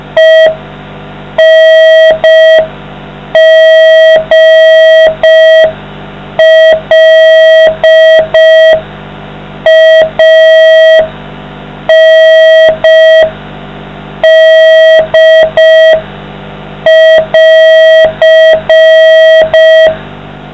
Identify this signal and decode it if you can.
Morse code, heard as a keyed tone, about 5 wpm, 640 Hz, with noise. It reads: ENGLAND<AR>